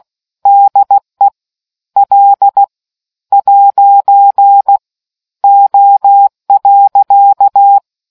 Morse code, heard as a keyed tone, about 16 wpm, spaced as for 12 wpm, 800 Hz. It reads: DE L'O.